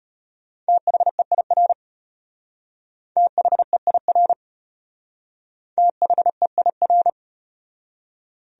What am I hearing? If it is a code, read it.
Morse code, heard as a keyed tone, about 34 wpm, 700 Hz. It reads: THEIR THEIR THEIR